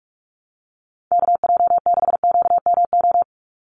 Morse code, heard as a keyed tone, about 44 wpm, 700 Hz. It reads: KJ6QMO
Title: USA Call Signs 45wpm